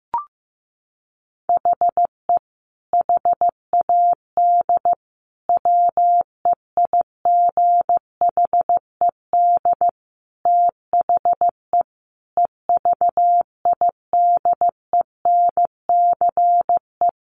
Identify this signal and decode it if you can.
Morse code, heard as a keyed tone, about 15 wpm, 700 Hz. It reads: HE HAD WEIGHED THE EVIDENCE
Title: A Sound Of Thunder 15wpm